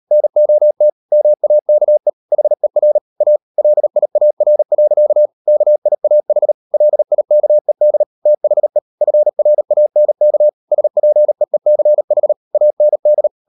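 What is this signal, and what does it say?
Morse code, heard as a keyed tone, about 38 wpm, 600 Hz. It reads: NOT MAKE HER A LIAR. KIAH LIKED THE FRANK SPEECH AND
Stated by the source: Morse code practice at 38 WPM, sending Ken Follett's novel Never in English, from CW med Gnister.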